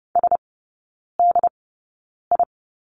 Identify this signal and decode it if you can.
Morse code, heard as a keyed tone, about 30 wpm, spaced as for 9 wpm, 700 Hz. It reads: SDI
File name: 081 Lesson 14e - Number 2 - Three Characters 30wpm